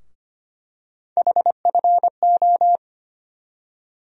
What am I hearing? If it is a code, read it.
Morse code, heard as a keyed tone, about 25 wpm, 700 Hz. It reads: HFO
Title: USA Call Sign Postfixes - 3 Letters 25wpm